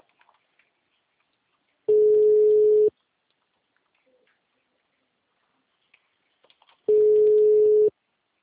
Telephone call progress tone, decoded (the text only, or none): ringback tone